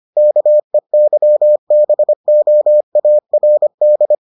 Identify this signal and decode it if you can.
Morse code, heard as a keyed tone, about 25 wpm, 600 Hz. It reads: KEYBOARD